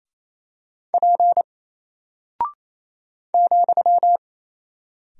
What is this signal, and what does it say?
Morse code, heard as a keyed tone, about 27 wpm, 700 Hz. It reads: P ,
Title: Single Character (Letter:Number:Punctuation:Prosign) 28wpm